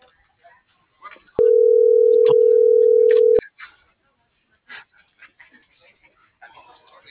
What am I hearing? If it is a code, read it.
Ringback tone (North America), 440 + 480 Hz.